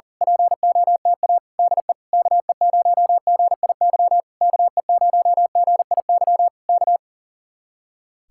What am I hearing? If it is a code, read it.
Morse code, heard as a keyed tone, about 37 wpm, 700 Hz. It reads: POTA DE KE0GIY KE0GIY K
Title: Speed-Racing Common QSO Phrases Encore - 40wpm